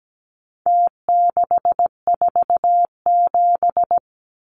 Morse code, heard as a keyed tone, about 17 wpm, 700 Hz. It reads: T647
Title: Numbers - Contest Style - 4-Digit 17wpm